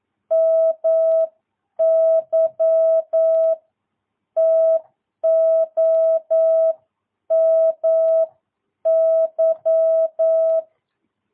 Morse code, heard as a keyed tone, about 9 wpm, spaced as for 8 wpm, 650 Hz. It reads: MY TOMY
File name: jh8cbh / 2022cw